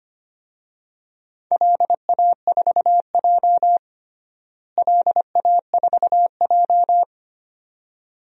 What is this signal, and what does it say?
Morse code, heard as a keyed tone, about 25 wpm, 700 Hz. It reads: LA4J LA4J
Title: DXCC-100 Call Signs - 4 Characters - Repeated 2x - 25wpm